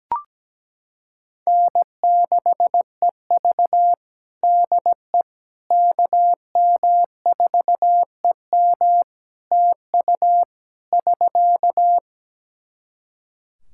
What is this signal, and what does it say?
Morse code, heard as a keyed tone, about 17 wpm, 700 Hz. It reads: N6EV DE KM4EM TU <SK>
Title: Practice QSO - Exchange 4 - Ending - 17wpm